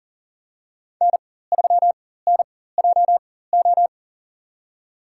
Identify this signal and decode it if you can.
Morse code, heard as a keyed tone, about 40 wpm, spaced as for 19 wpm, 700 Hz. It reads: N3NJO
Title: USA Call Signs - 5 Characters - 4x Character Spacing - 40wpm